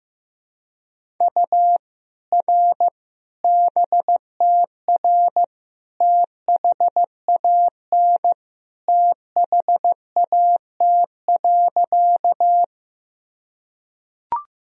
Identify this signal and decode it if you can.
Morse code, heard as a keyed tone, about 15 wpm, 700 Hz. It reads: U R BTR THAN THAT.
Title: Sentences from Top 400 Words and Abbreviations 15wpm